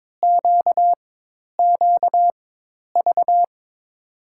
Morse code, heard as a keyed tone, about 22 wpm, spaced as for 10 wpm, 700 Hz. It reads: QQV
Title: USA Call Signs - 5 Characters - 4x Character Spacing - 22wpm